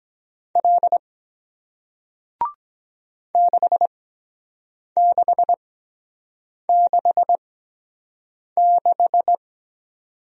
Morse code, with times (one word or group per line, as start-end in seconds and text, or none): 0.6-1.0 L
3.3-9.3 6666